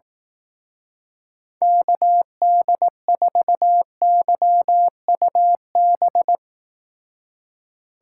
Morse code, heard as a keyed tone, about 18 wpm, 700 Hz. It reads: KD4YUB
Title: Speed Racing - USA 2x3 Call Signs 15wpm